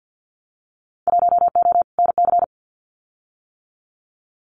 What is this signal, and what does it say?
Morse code, heard as a keyed tone, about 55 wpm, 700 Hz. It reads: 1O NC
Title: Winter Field Day Exchanges 50wpm